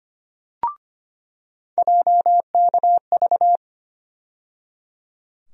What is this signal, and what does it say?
Morse code, heard as a keyed tone, about 25 wpm, 700 Hz. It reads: JKV
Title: USA Call Sign Postfixes - 3 Letters 25wpm